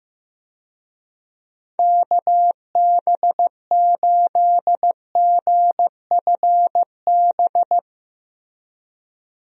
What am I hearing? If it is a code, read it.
Morse code, heard as a keyed tone, about 15 wpm, 700 Hz. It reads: KB8GFB
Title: USA Call Signs - 6 Characters - Repeated 2x - 15wpm